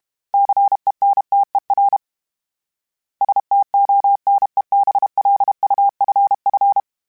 Morse code, heard as a keyed tone, about 32 wpm, 800 Hz. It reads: CENTER STONEBLUFF
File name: fromSMs / FromSMs mp3s / Cities@30